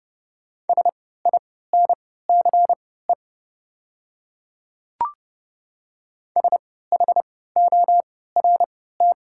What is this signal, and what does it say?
Morse code, heard as a keyed tone, about 30 wpm, spaced as for 17 wpm, 700 Hz. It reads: SINCE SHORT
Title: 3x Character Spacing - Top 500 Words - 30wpm